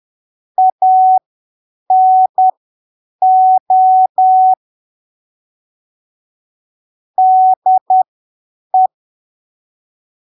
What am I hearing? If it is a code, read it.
Morse code, heard as a keyed tone, about 10 wpm, spaced as for 7 wpm, 750 Hz. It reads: ANO DE